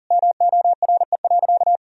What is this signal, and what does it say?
Morse code, heard as a keyed tone, about 40 wpm, 700 Hz. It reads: MORE.
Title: Sentences from Top 1000 Words - Review Part 4 - 40wpm